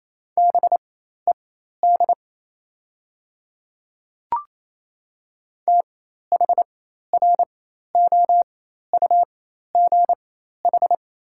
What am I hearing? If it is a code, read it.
Morse code, heard as a keyed tone, about 28 wpm, spaced as for 13 wpm, 700 Hz. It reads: BED THROUGH
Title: 4x Character Spacing - Top 500 Words - 28wpm